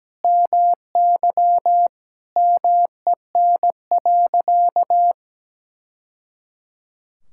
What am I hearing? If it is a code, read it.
Morse code, heard as a keyed tone, about 17 wpm, 700 Hz. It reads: MY MEN.
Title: Sentences from Top 1000 Words - Review Part 6 - 17wpm